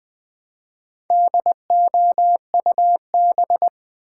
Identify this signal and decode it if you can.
Morse code, heard as a keyed tone, about 20 wpm, 700 Hz. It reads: DOUB
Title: Build A Word - 1 Letter At A Time - 3 to 5-Letter Words - 20wpm